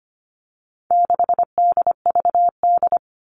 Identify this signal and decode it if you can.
Morse code, heard as a keyed tone, about 25 wpm, 700 Hz. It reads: 6DVD
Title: All Cut Numbers - 4-Digit 25wpm